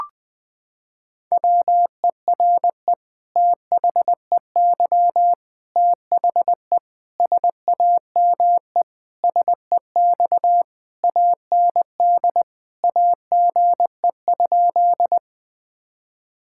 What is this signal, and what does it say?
Morse code, heard as a keyed tone, about 20 wpm, 700 Hz. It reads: WERE THEY THE SAME SEX AND AGE?
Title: Sentences from Top 1600 Words 20wpm